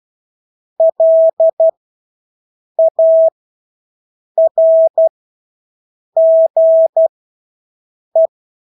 Morse code, heard as a keyed tone, about 12 wpm, spaced as for 6 wpm, 650 Hz. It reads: LARGE
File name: Engelske ord 12-6 wpm